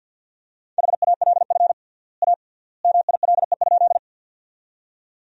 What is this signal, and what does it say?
Morse code, heard as a keyed tone, about 47 wpm, 700 Hz. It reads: HALF A MILE?
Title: 3x Word-Spacing - Sentences from Top 400 Words - 50wpm